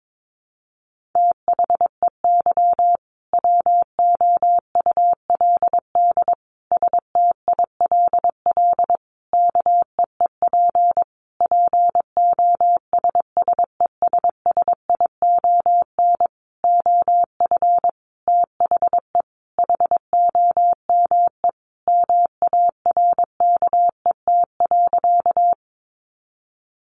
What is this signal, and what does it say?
Morse code, heard as a keyed tone, about 22 wpm, 700 Hz. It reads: THEY WOULD STILL KEEP POSSESSION OF THE HOME MARKET.
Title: Sentences from Top 1400 Words 22wpm